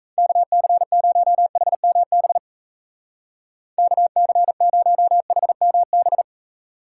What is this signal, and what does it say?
Morse code, heard as a keyed tone, about 34 wpm, 700 Hz. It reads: KC0HMB KC0HMB
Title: Speed Racing - USA 2x3 Call Signs 28wpm